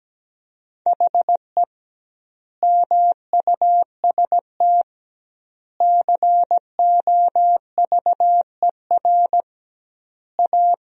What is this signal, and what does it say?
Morse code, heard as a keyed tone, about 17 wpm, 700 Hz. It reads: HE MUST COVER A
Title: 2x Word-Spacing - Sentences from Top 500 Words - 17wpm